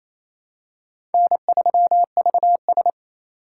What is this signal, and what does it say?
Morse code, heard as a keyed tone, about 28 wpm, 700 Hz. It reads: N3VS
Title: USA Call Signs - 2x Character Spacing - 28wpm